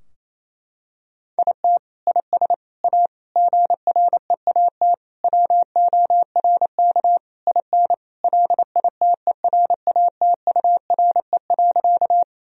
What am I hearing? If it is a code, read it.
Morse code, heard as a keyed tone, about 28 wpm, 700 Hz. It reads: IT IS A GREAT WORK IN LITERATURE.
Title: Sentences from Top 1000 Words - Review Part 3 - 28wpm